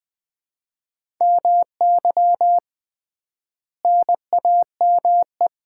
Morse code, heard as a keyed tone, about 20 wpm, 700 Hz. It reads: MY NAME
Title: Common QSO Phrases - 3x Word-Spacing - 20wpm